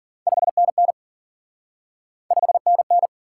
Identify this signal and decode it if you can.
Morse code, heard as a keyed tone, about 46 wpm, 700 Hz. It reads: 5NN 5NN
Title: Speed-Racing Top 90 QSO Elements 40wpm